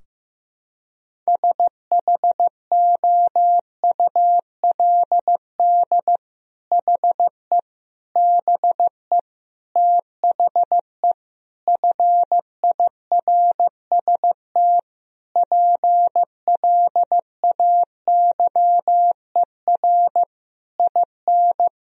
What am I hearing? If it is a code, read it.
Morse code, heard as a keyed tone, about 15 wpm, 700 Hz. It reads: SHOULD HE BE THE FIRST PLAYER IN
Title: Sentences from Top 1000 Words - Review Part 4 - 15wpm